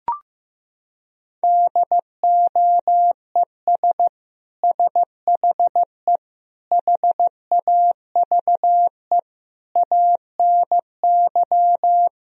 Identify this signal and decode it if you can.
Morse code, heard as a keyed tone, about 16 wpm, 700 Hz. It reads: DOES SHE HAVE ANY
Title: Sentences from Top 1000 Words - Review Part 4 - 15wpm